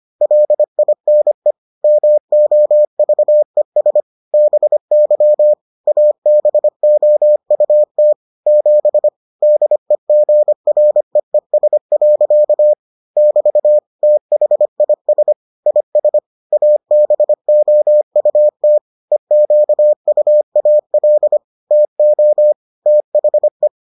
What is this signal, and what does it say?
Morse code, heard as a keyed tone, about 25 wpm, 600 Hz. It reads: LINE MOVES BY ABOUT 7 DEGREES. <BT> THIS IS ABOUT EQUAL TO THE